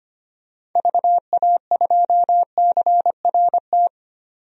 Morse code, heard as a keyed tone, about 25 wpm, 700 Hz. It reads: VA2CRT
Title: DXCC-100 Call Signs - 6 Characters - 25wpm